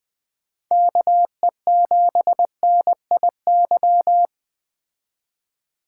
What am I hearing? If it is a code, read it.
Morse code, heard as a keyed tone, about 20 wpm, 700 Hz. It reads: KE7NIY
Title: Speed Racing - USA 2x3 Call Signs 17wpm